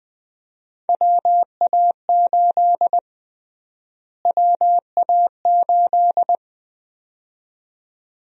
Morse code, heard as a keyed tone, about 20 wpm, 700 Hz. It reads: WA8 WA8
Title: USA Call Sign Prefixes - Repeated 2x - 20wpm